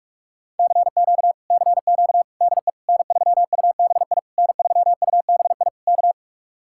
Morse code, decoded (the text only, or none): CQ CQ DE N3UBI N3UBI K